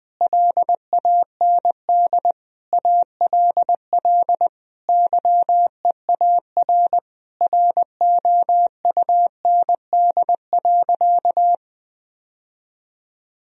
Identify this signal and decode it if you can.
Morse code, heard as a keyed tone, about 20 wpm, 700 Hz. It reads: LAND ALL YEAR ROUND.
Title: Sentences from Top 1000 Words - Review Part 1 - 20wpm